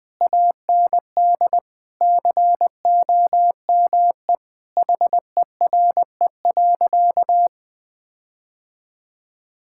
Morse code, heard as a keyed tone, about 20 wpm, 700 Hz. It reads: AND COME HERE.
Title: Sentences from Top 1600 Words 20wpm